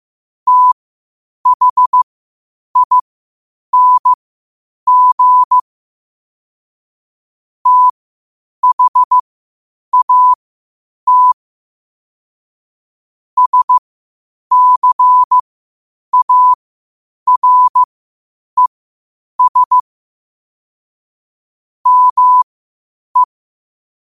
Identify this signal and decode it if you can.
Morse code, heard as a keyed tone, about 15 wpm, spaced as for 8 wpm, 1000 Hz. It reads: THING THAT SCARES ME
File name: qotdfiles / 8wpm